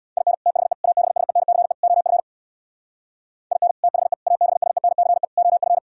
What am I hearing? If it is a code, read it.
Morse code, heard as a keyed tone, about 60 wpm, 700 Hz. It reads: IN THE ABSENCE OF IN THE ABSENCE OF
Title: Speed-Racing - Sets of 4 Words 50wpm